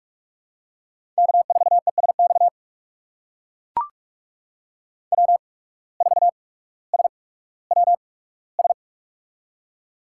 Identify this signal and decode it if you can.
Morse code, heard as a keyed tone, about 44 wpm, spaced as for 19 wpm, 700 Hz. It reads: K4ESX W4SWS
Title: USA Call Signs - 5 Characters - 8x Character Spacing - 45wpm